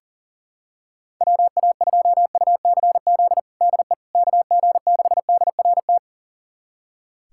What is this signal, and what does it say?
Morse code, heard as a keyed tone, about 40 wpm, 700 Hz. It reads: WA2UCZ DE KG6DRT
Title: USA Callsign DE Callsign - 6 Characters - 40wpm